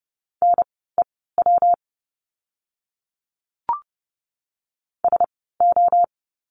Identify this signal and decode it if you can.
Morse code, heard as a keyed tone, about 30 wpm, spaced as for 17 wpm, 700 Hz. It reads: NEW SO
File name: course / 30:10wpm / 230 Lesson 37f - Letter x - Words 30wpm